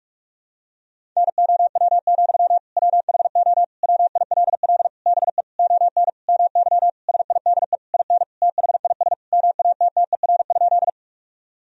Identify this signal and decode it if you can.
Morse code, heard as a keyed tone, about 46 wpm, 700 Hz. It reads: NOW, WHO WILL BE ON MY SIDE IN THIS MATTER?
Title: Sentences from Top 1000 Words - Review Part 1 - 45wpm